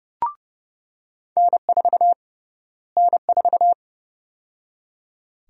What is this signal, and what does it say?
Morse code, heard as a keyed tone, about 30 wpm, 700 Hz. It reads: N4 N4
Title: USA Call Sign Prefixes - Repeated 2x - 30wpm